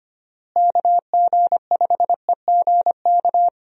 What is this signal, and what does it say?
Morse code, heard as a keyed tone, about 25 wpm, 700 Hz. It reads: KG5EGK
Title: USA Call Signs - 6 Characters - Repeated 2x - 25wpm